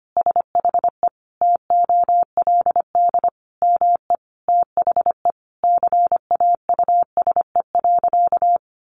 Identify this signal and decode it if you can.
Morse code, heard as a keyed tone, about 25 wpm, 700 Hz. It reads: SHE TOLD ME THE CAUSE.
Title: Sentences from Top 700 Words 25wpm